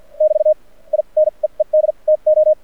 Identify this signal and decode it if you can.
Morse code, heard as a keyed tone, about 50 wpm, spaced as for 38 wpm, 620 Hz. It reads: <BT> I NEED TO
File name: archived / CRSnet